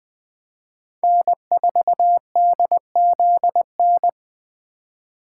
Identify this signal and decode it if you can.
Morse code, heard as a keyed tone, about 21 wpm, 700 Hz. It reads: N4DZN
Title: Speed Racing - USA 1x3 Call Signs 17wpm